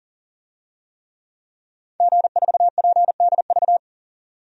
Morse code, heard as a keyed tone, about 40 wpm, 700 Hz. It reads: G4PDV
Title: DXCC-100 Call Signs - 5 Characters - 40wpm